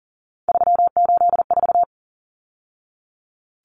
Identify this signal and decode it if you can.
Morse code, heard as a keyed tone, about 40 wpm, 700 Hz. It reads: 384
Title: Numbers - 3-Digit 40wpm